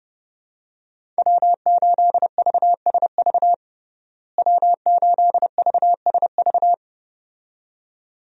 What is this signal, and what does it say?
Morse code, heard as a keyed tone, about 32 wpm, 700 Hz. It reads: W8VSV W8VSV
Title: USA Call Signs - 5 Characters - Repeated 2x - 30wpm